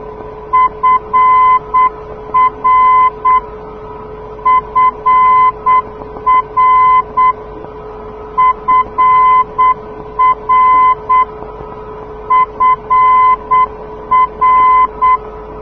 Morse code, heard as a keyed tone, about 8 wpm, 1020 Hz. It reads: FR FR FR FR
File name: europe / d-6028 - ndb frankfurt fr-297 (2004) (rainer brannolte, d) xc